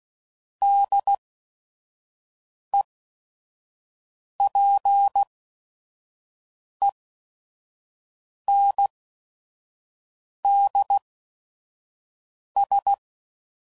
Morse code, heard as a keyed tone, about 16 wpm, spaced as for 5 wpm, 800 Hz. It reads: DEPENDS
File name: CW Code Groups @5WPM / Technical Subjects / SwPwrSup